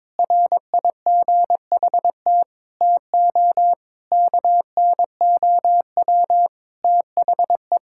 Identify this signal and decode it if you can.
Morse code, heard as a keyed tone, about 22 wpm, 700 Hz. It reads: RIGHT TO KNOW THE
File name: Sentences with 7-Word Phrases - 22wpm